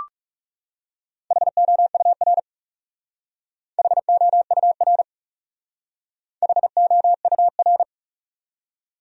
Morse code, heard as a keyed tone, about 40 wpm, 700 Hz. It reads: HOUR HOUR HOUR